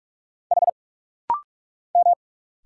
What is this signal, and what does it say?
Morse code, heard as a keyed tone, about 44 wpm, 690 Hz. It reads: H M
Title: Single Letter-Number - Rapid-Fire 45wpm